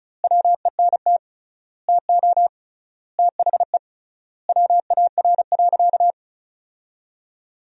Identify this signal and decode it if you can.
Morse code, heard as a keyed tone, about 36 wpm, 700 Hz. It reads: WENT TO THE WAR.
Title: 3x Word-Spacing - Sentences from Top 400 Words - 35wpm